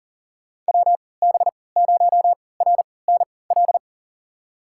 Morse code, heard as a keyed tone, about 40 wpm, spaced as for 23 wpm, 700 Hz. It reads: WB0RNL
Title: USA Call Signs - 3x Character Spacing - 40wpm